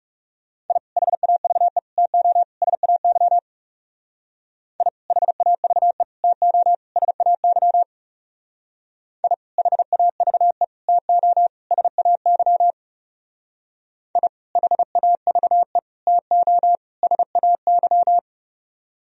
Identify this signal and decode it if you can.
Morse code, heard as a keyed tone, about 37 wpm, 700 Hz. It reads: I HAVE TO SAY I HAVE TO SAY I HAVE TO SAY I HAVE TO SAY
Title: Speed-Racing - Sets of 4 Words 30wpm